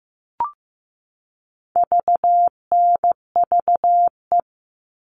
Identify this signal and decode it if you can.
Morse code, heard as a keyed tone, about 15 wpm, 700 Hz. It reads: VNVE
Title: All Cut Numbers - 4-Digit 15wpm